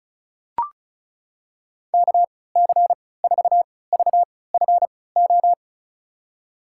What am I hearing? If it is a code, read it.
Morse code, heard as a keyed tone, about 34 wpm, spaced as for 20 wpm, 700 Hz. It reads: KC4VFO
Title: USA Call Signs - 6 Characters - 3x Character Spacing - 35wpm